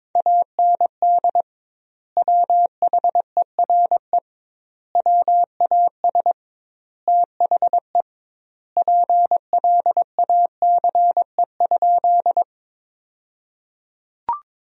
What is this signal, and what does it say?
Morse code, heard as a keyed tone, about 22 wpm, 700 Hz. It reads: AND WHERE WAS THE PLACE?
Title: 2x Word-Spacing - Sentences from Top 200 Words - 22wpm